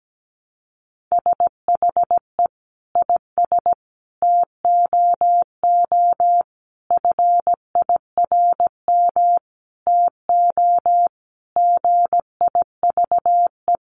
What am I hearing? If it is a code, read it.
Morse code, heard as a keyed tone, about 17 wpm, 700 Hz. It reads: SHE IS TOO FIRM TO GIVE
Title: Sentences from Top 1400 Words 17wpm